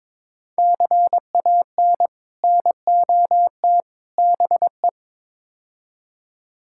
Morse code, heard as a keyed tone, about 23 wpm, 700 Hz. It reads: CAN NOT BE
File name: practice / 22wpm / Build A Phrase - 5-Word Phrases - 22wpm